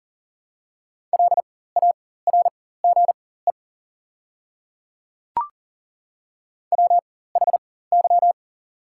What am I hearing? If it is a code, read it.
Morse code, heard as a keyed tone, about 40 wpm, spaced as for 19 wpm, 700 Hz. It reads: LARGE WHY
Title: 4x Character Spacing - Top 300 Words - 40wpm